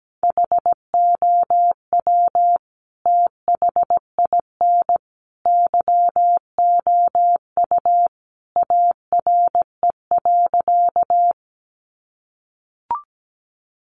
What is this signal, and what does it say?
Morse code, heard as a keyed tone, about 17 wpm, 700 Hz. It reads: HOW THIN YOU ARE.
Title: Sentences from Top 1200 Words 17wpm